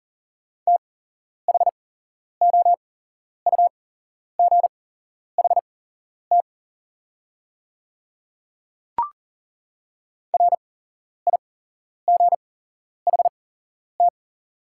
Morse code, heard as a keyed tone, about 40 wpm, spaced as for 11 wpm, 690 Hz. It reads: THOUGHT RIGHT
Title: 8x Character Spacing - Top 500 Words - 40wpm